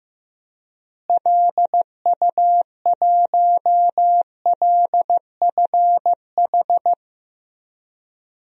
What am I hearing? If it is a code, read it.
Morse code, heard as a keyed tone, about 15 wpm, 700 Hz. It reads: LU1LFH
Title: DXCC-100 Call Signs - 6 Characters - Repeated 3x - 15wpm